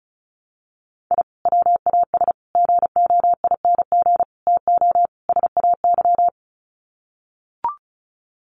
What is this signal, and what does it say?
Morse code, heard as a keyed tone, about 34 wpm, 700 Hz. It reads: I WAS GOING TO SAY
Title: Sets of 5 Words 35wpm